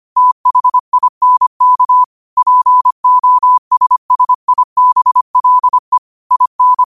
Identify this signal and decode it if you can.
Morse code, heard as a keyed tone, about 25 wpm, 1000 Hz. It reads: THINK POSSIBLE IN
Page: Quotes for Fri, 15 Aug 2025 in Morse Code at 25 words per minute.